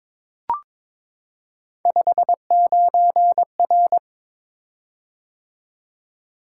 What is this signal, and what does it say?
Morse code, heard as a keyed tone, about 23 wpm, 700 Hz. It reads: 59R
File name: practice / 22wpm / Three Letter-Number Combinations - 22wpm